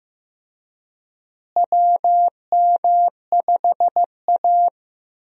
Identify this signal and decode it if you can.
Morse code, heard as a keyed tone, about 15 wpm, 700 Hz. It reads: WM5A
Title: USA Call Signs - 4 Characters - Repeated 3x - 15wpm